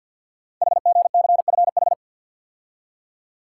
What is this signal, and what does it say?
Morse code, heard as a keyed tone, about 50 wpm, 690 Hz. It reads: HGCFH
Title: Code Groups - 5 Letters - 50wpm